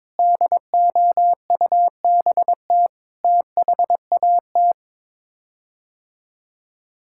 Morse code, heard as a keyed tone, about 22 wpm, 700 Hz. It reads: DOUBT THAT
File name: practice / 22wpm / Sets of 7 Words - 22wpm